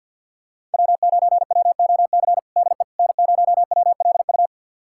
Morse code, heard as a keyed tone, about 52 wpm, 700 Hz. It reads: W9WOC DE N0WLU
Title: USA Callsign DE Callsign - 5 Characters - 50wpm